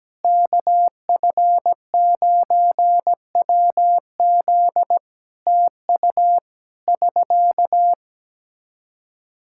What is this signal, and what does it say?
Morse code, heard as a keyed tone, about 17 wpm, 700 Hz. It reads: KF9WZ TU <SK>
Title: K1USN SST QSO Practice 17wpm